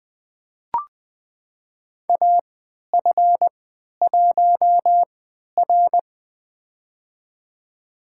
Morse code, heard as a keyed tone, about 20 wpm, spaced as for 11 wpm, 700 Hz. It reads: AF1R
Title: USA Call Signs - 4 Characters - 3x Character Spacing - 20wpm